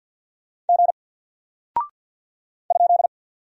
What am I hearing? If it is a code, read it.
Morse code, heard as a keyed tone, about 48 wpm, 700 Hz. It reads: G ?
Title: Single Character (Letter:Number:Punctuation:Prosign) Alt 50wpm